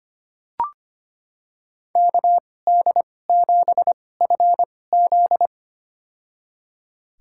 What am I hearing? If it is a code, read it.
Morse code, heard as a keyed tone, about 25 wpm, spaced as for 18 wpm, 700 Hz. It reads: KD7FZ
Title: USA Call Signs - 5 Characters - 2x Character Spacing - 25wpm